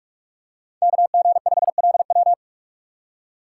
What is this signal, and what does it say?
Morse code, heard as a keyed tone, about 44 wpm, 700 Hz. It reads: KG5LW
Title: USA Call Signs - 5 Characters - 6x Character Spacing - 45wpm